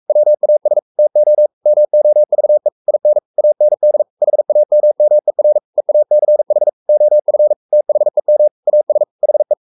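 Morse code, heard as a keyed tone, about 44 wpm, 600 Hz. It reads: WAS TO MOVE IN AND HAMMER EACH OF THEM AS HE
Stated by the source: Morse code at 43 wpm sending Ken Follett's novel Never in English, from CW med Gnister.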